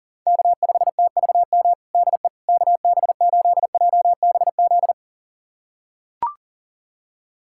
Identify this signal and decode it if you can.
Morse code, heard as a keyed tone, about 40 wpm, 700 Hz. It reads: K5TVM DE KB8JBZ
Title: USA Callsign DE Callsign 40wpm